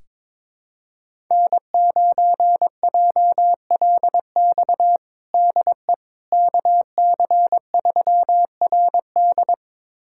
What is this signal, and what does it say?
Morse code, heard as a keyed tone, about 22 wpm, 700 Hz. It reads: N9JLX DE KC3RD
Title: USA Callsign DE Callsign 22wpm